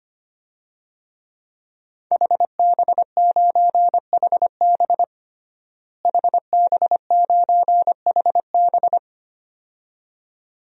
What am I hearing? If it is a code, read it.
Morse code, heard as a keyed tone, about 25 wpm, 700 Hz. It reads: HB9HB HB9HB
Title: DXCC-100 Call Signs - 5 Characters - Repeated 2x - 25wpm